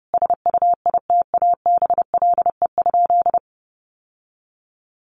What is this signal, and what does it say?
Morse code, heard as a keyed tone, about 28 wpm, 700 Hz. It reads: SUITABLE?
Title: Sentences from Top 1400 Words 30wpm